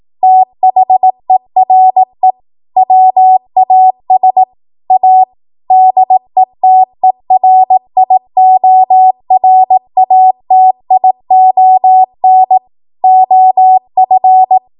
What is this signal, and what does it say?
Morse code, heard as a keyed tone, about 18 wpm, 750 Hz. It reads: THERE WAS A DETERIORATION OF
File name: Archive / 18 WPM / 18WPM